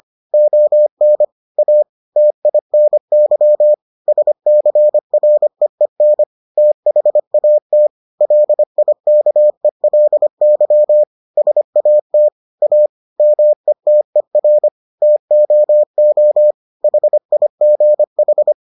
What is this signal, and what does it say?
Morse code, heard as a keyed tone, about 25 wpm, 600 Hz. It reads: ON A TINY SCREEN THAT LIKELY SAT A METER TOO HIGH